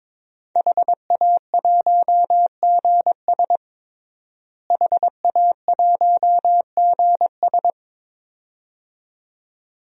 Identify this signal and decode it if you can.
Morse code, heard as a keyed tone, about 22 wpm, 700 Hz. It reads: HA1GS HA1GS
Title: DXCC-100 Call Signs - 5 Characters - Repeated 2x - 22wpm